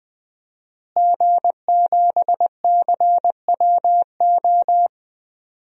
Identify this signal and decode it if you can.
Morse code, heard as a keyed tone, about 20 wpm, 700 Hz. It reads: G7CWO
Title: DXCC-100 Call Signs - 5 Characters - Repeated 2x - 20wpm